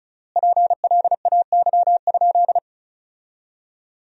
Morse code, heard as a keyed tone, about 36 wpm, 700 Hz. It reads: PLAY?
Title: 3x Word-Spacing - Sentences from Top 300 Words - 35wpm